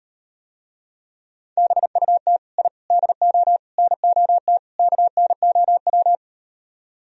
Morse code, heard as a keyed tone, about 38 wpm, 700 Hz. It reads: BUT I DO NOT KNOW